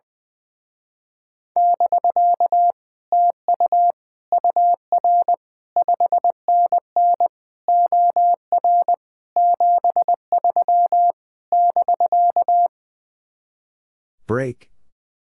Morse code, heard as a keyed tone, about 20 wpm, 700 Hz. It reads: <BK> TU UR 5NN OR 73 <BK>
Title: Speed-Racing Common QSO Phrases - 20wpm